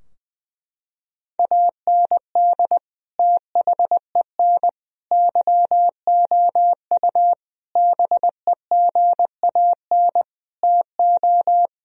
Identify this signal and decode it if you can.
Morse code, heard as a keyed tone, about 20 wpm, 700 Hz. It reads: AND THEN YOU BEGAN TO
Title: Sentences from Top 1000 Words - Review Part 2 - 20wpm